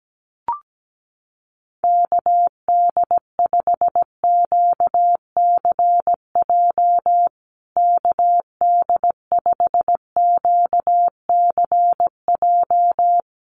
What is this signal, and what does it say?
Morse code, heard as a keyed tone, about 17 wpm, 700 Hz. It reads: KD5QCJ KD5QCJ
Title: USA Call Signs Repeated 3x 17wpm